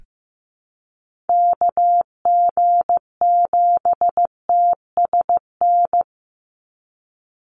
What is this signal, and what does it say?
Morse code, heard as a keyed tone, about 15 wpm, 700 Hz. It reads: KG7TSN
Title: USA Call Signs 15wpm